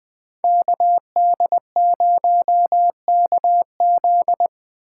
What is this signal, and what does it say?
Morse code, heard as a keyed tone, about 20 wpm, 700 Hz. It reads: KD0KZ
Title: Speed Racing - USA 2x2 Call Signs 17wpm